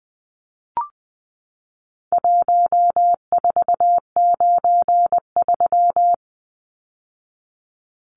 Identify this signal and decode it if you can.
Morse code, heard as a keyed tone, about 20 wpm, 700 Hz. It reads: 1493
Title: Numbers - Contest Style - 4-Digit 20wpm